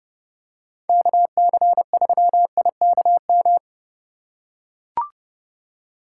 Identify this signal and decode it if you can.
Morse code, heard as a keyed tone, about 30 wpm, 700 Hz. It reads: KC3IKM